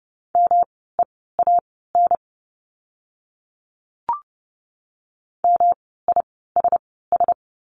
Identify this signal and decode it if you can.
Morse code, heard as a keyed tone, about 30 wpm, spaced as for 17 wpm, 700 Hz. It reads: MEAN MISS
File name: course / 30:10wpm / 108 Lesson 18f - Letter m - Words 30wpm